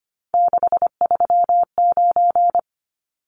Morse code, heard as a keyed tone, about 25 wpm, 700 Hz. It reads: 639
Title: Numbers - 3-Digit 25wpm